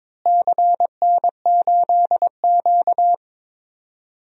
Morse code, heard as a keyed tone, about 22 wpm, 700 Hz. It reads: CN8Q
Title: DXCC-100 Call Signs - 4 Characters - Repeated 2x - 22wpm